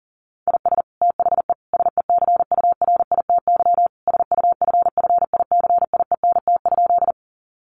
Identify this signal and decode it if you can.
Morse code, heard as a keyed tone, about 37 wpm, 700 Hz. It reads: IS THE SECURITY SUFFICIENT?
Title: Sentences from Top 1400 Words 40wpm